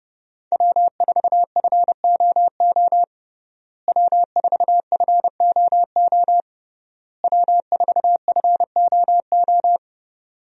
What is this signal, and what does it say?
Morse code, heard as a keyed tone, about 32 wpm, 700 Hz. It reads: W4FOO W4FOO W4FOO
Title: USA Call Signs - 5 Characters - Repeated 3x - 30wpm